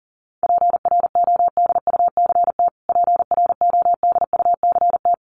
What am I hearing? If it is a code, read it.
Morse code, heard as a keyed tone, about 40 wpm, 700 Hz. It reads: PRODUCT PRODUCT
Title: Top 500 Words Repeated 40wpm